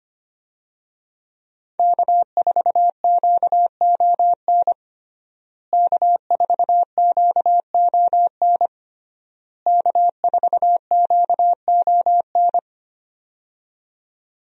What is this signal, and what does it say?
Morse code, heard as a keyed tone, about 25 wpm, 700 Hz. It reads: K4QON K4QON K4QON
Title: USA Call Signs - 5 Characters - Repeated 3x - 25wpm